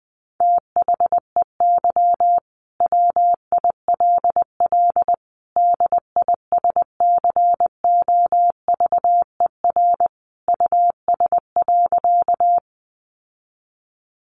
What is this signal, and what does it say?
Morse code, heard as a keyed tone, about 20 wpm, 700 Hz. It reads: THEY WILL DISCOVER US.
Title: Sentences from Top 1100 Words 20wpm